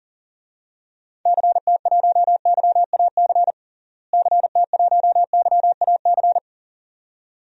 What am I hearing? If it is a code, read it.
Morse code, heard as a keyed tone, about 40 wpm, 700 Hz. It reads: CT1YAC CT1YAC
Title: DXCC-100 Call Signs - 6 Characters - Repeated 2x - 40wpm